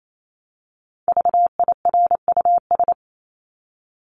Morse code, heard as a keyed tone, about 28 wpm, 700 Hz. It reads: VIRUS
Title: Top 1251-1500 Words 28wpm